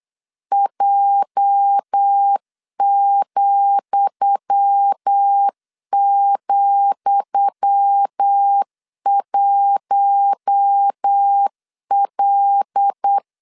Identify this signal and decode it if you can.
Morse code, heard as a keyed tone, about 8 wpm, 800 Hz. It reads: J,,1L
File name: LEZIONI / 10 SIGNS / 10D COMA 8wpm 3749